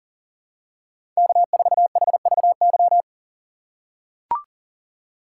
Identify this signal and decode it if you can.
Morse code, heard as a keyed tone, about 38 wpm, 700 Hz. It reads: K4HVY
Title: USA Call Signs - 5 Characters - 4x Character Spacing - 40wpm